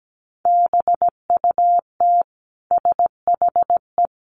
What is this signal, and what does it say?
Morse code, heard as a keyed tone, about 17 wpm, 700 Hz. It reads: BUT SHE
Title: Sentences from Top 1100 Words 17wpm